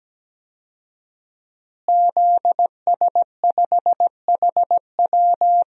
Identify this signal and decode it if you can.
Morse code, heard as a keyed tone, about 17 wpm, 700 Hz. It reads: ZS5HW
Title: DXCC-100 Call Signs - 5 Characters - Repeated 3x - 17wpm